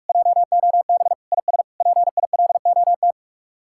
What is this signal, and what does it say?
Morse code, heard as a keyed tone, about 66 wpm, spaced as for 55 wpm, 700 Hz. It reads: JOB IS PILOT
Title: Speed-Racing Common QSO Phrases Encore - 45wpm